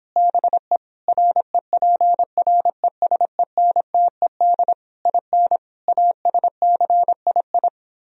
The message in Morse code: BE REPRESENTED IN ASCII

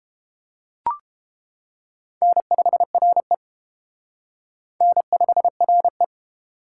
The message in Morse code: N5RE N5RE

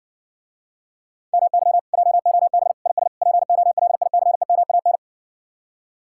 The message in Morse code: MY JOB IS POLICEMAN

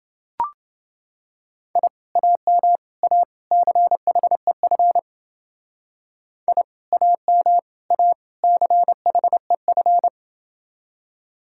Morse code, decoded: I AM A CHEF I AM A CHEF

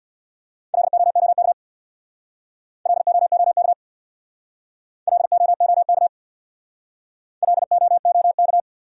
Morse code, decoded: LOOK LOOK LOOK LOOK

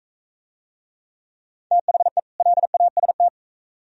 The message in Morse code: THE LAST

